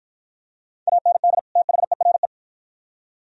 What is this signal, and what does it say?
Morse code, heard as a keyed tone, about 49 wpm, 700 Hz. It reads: AND THERE